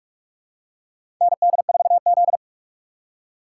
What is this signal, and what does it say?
Morse code, heard as a keyed tone, about 45 wpm, 700 Hz. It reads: ND4Z
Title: USA Call Signs - 4x Character Spacing - 45wpm